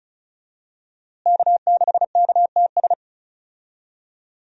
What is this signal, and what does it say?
Morse code, heard as a keyed tone, about 35 wpm, 700 Hz. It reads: K6KTS
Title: USA Call Signs - 5 Characters - Repeated 2x - 35wpm